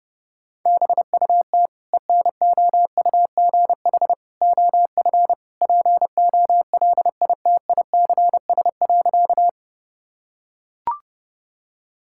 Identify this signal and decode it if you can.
Morse code, heard as a keyed tone, about 30 wpm, 700 Hz. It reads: BUT ENOUGH OF POLITICS.
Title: Sentences from Top 1000 Words - Review Part 6 - 30wpm